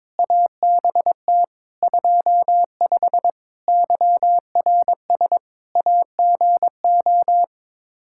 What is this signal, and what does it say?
Morse code, heard as a keyed tone, about 22 wpm, 700 Hz. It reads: ABT 25 YRS AGO
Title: Practice QSO - Exchange 3 - More Info - 22wpm